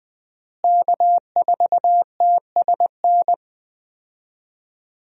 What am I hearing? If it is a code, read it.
Morse code, heard as a keyed tone, about 20 wpm, 700 Hz. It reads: K4TSN
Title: USA Call Signs - 5 Characters - Repeated 2x - 20wpm